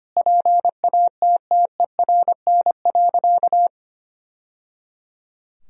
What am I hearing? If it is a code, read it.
Morse code, heard as a keyed tone, about 25 wpm, 700 Hz. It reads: PATTERN.
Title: Sentences from Top 1000 Words - Review Part 2 - 25wpm